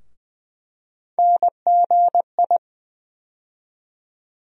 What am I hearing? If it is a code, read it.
Morse code, heard as a keyed tone, about 20 wpm, 700 Hz. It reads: NGI
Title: USA Call Sign Postfixes - 3 Letters 20wpm